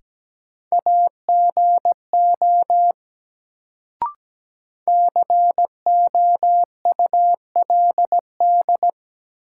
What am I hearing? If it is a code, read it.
Morse code, heard as a keyed tone, about 17 wpm, 700 Hz. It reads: AGO COULD
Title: Sets of 2 Words - New Word Drill - 17wpm